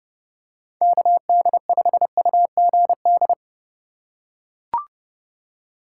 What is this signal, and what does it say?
Morse code, heard as a keyed tone, about 31 wpm, 700 Hz. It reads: KD5UGD
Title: USA Call Signs - 6 Characters - 3x Character Spacing - 30wpm